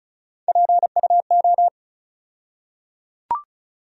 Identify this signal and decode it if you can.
Morse code, heard as a keyed tone, about 36 wpm, 700 Hz. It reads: PUO
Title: USA Call Sign Postfixes - 3 Letters 35wpm